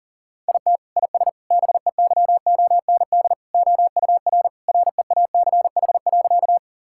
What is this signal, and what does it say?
Morse code, heard as a keyed tone, about 40 wpm, 700 Hz. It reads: IT IS BEYOND OUR REACH.
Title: Sentences from Top 1000 Words - Review Part 6 - 40wpm